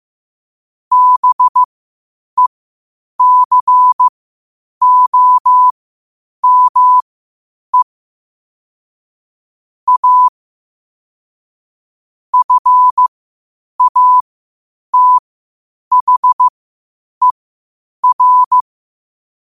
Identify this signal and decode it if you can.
Morse code, heard as a keyed tone, about 15 wpm, spaced as for 8 wpm, 1000 Hz. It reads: BECOME A FATHER
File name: 8wpm